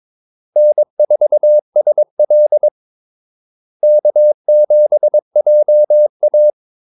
Morse code, heard as a keyed tone, about 22 wpm, 600 Hz. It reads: N4SL K7JA
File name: Callsigns-22wpm